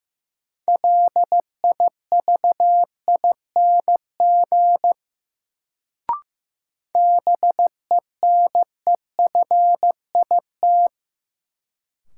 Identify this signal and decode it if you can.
Morse code, heard as a keyed tone, about 15 wpm, 700 Hz. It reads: LIVING BENEFIT
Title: Sentences from Top 1000 Words - Review Part 6 - New Word Drill - 15wpm